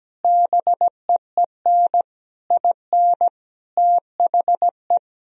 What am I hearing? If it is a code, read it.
Morse code, heard as a keyed tone, about 17 wpm, 700 Hz. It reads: BEEN IN THE
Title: Sentences from Top 1000 Words - Review Part 1 - 17wpm